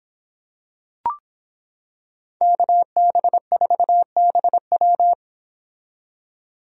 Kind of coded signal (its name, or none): Morse code